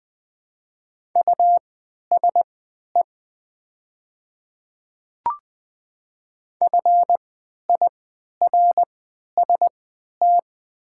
Morse code, heard as a keyed tone, about 20 wpm, spaced as for 11 wpm, 700 Hz. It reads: USE FIRST
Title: 3x Character Spacing - Top 100 Words - 20wpm